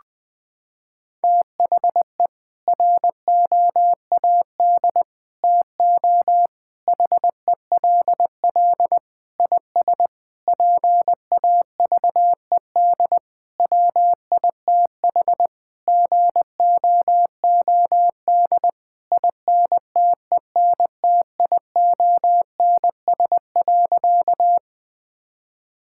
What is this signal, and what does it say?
Morse code, heard as a keyed tone, about 20 wpm, 700 Hz. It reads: THE ROAD TO HELL IS PAVED WITH GOOD INTENTIONS.
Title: Sentences with 8-Word Phrases - 20wpm